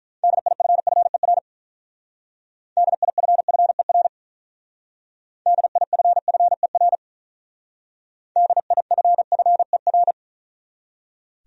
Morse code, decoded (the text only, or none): DIFFER DIFFER DIFFER DIFFER